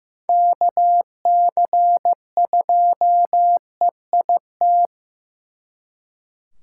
Morse code, heard as a keyed tone, about 18 wpm, 700 Hz. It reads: KC2EIT
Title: Speed Racing - USA 2x3 Call Signs 15wpm